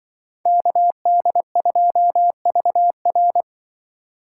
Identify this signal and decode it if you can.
Morse code, heard as a keyed tone, about 25 wpm, 700 Hz. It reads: KD2VR
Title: Speed Racing - USA 2x2 Call Signs 20wpm